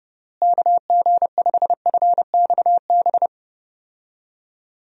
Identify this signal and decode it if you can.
Morse code, heard as a keyed tone, about 30 wpm, 700 Hz. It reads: KG5FXB
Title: USA Call Signs - 6 Characters - Repeated 3x - 30wpm